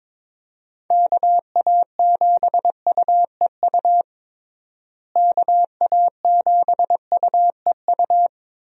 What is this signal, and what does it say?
Morse code, heard as a keyed tone, about 22 wpm, 700 Hz. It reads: KA7UEU KA7UEU
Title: USA Call Signs - 6 Characters - Repeated 2x - 22wpm